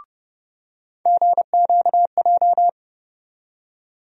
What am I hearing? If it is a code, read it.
Morse code, heard as a keyed tone, about 30 wpm, 700 Hz. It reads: GQJ
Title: USA Call Sign Postfixes - 3 Letters 30wpm